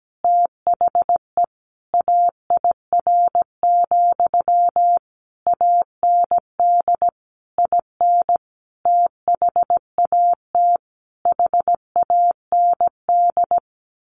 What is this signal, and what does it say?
Morse code, heard as a keyed tone, about 17 wpm, 700 Hz. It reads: THE AIR, AND IN THAT HAND
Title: A Sound Of Thunder 17wpm